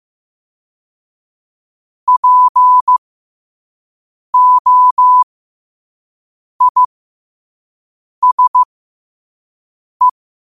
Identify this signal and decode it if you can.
Morse code, heard as a keyed tone, about 15 wpm, spaced as for 5 wpm, 1000 Hz. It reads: POISE
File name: qotdfiles / 5wpm